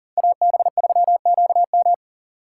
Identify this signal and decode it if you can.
Morse code, heard as a keyed tone, about 40 wpm, 700 Hz. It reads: AB3QM
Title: USA Call Signs - 5 Characters - 2x Character Spacing - 40wpm